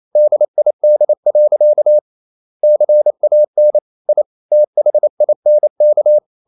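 Morse code, heard as a keyed tone, about 28 wpm, 600 Hz. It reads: DID. CAN I THINK
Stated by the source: Morse code practice at 28 wpm, text of Ken Follett's novel Never in English, from CW med Gnister.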